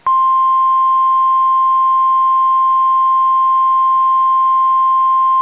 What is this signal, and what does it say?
Emergency Alert System (SAME) tones (the 1050 Hz attention tone). It is heard from 0.8 to 5.1 s.